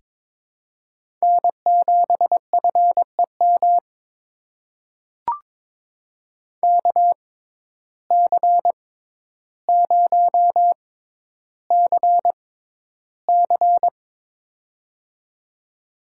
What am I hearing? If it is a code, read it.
Morse code, heard as a keyed tone, about 22 wpm, spaced as for 11 wpm, 700 Hz. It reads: N7FEM KC0CC
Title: USA Call Signs - 5 Characters - 6x Character Spacing - 22wpm